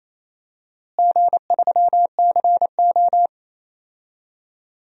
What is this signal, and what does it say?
Morse code, heard as a keyed tone, about 28 wpm, 700 Hz. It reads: G3CO
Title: DXCC-100 Call Signs - 4 Characters - Repeated 2x - 28wpm